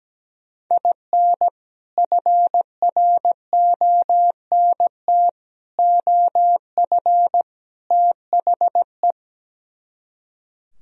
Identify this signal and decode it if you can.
Morse code, heard as a keyed tone, about 17 wpm, 700 Hz. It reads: IN FRONT OF THE
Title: Speed-Racing - Sets of 4 Words 17wpm